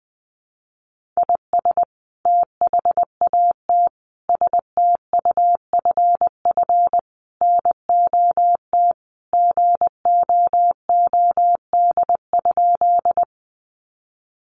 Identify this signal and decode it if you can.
Morse code, heard as a keyed tone, about 20 wpm, 700 Hz. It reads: IS THAT STUFF NOT GOOD?
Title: Sentences from Top 1400 Words 20wpm